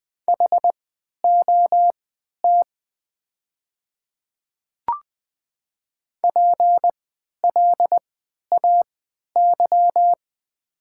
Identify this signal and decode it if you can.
Morse code, heard as a keyed tone, about 21 wpm, spaced as for 12 wpm, 700 Hz. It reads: HOT PLAY
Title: 3x Character Spacing - Top 200 Words - 20wpm